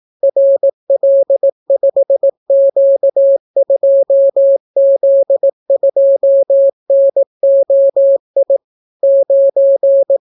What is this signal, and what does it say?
Morse code, heard as a keyed tone, about 18 wpm, 550 Hz. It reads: RL5Q2Z2NOI 9